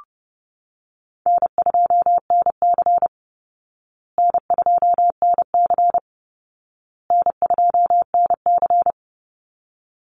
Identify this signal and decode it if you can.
Morse code, heard as a keyed tone, about 30 wpm, 700 Hz. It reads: N2NC N2NC N2NC
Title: Call Signs Repeated 2x - 30wpm